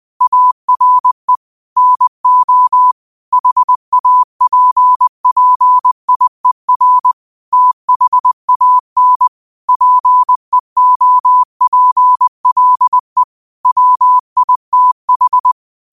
Morse code, heard as a keyed tone, about 20 wpm, 1000 Hz. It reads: ARE NO HAPPIER THAN PEOPLE WITH
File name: qotdfiles / 20wpm